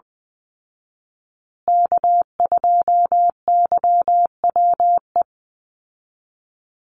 Morse code, heard as a keyed tone, about 20 wpm, 700 Hz. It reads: K2YWE
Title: Call Signs Repeated 2x - 20wpm